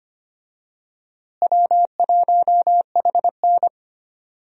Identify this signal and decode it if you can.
Morse code, heard as a keyed tone, about 25 wpm, 700 Hz. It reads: W1HN
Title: USA Call Signs - 4 Characters - Repeated 3x - 25wpm